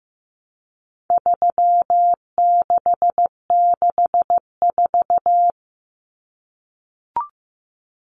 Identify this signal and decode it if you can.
Morse code, heard as a keyed tone, about 15 wpm, 700 Hz. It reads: 3664
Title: Common Cut Numbers - 4-Digit 15wpm